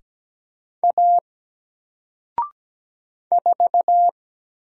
Morse code, heard as a keyed tone, about 17 wpm, 700 Hz. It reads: A 4
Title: Single Letter-Number Alt 17wpm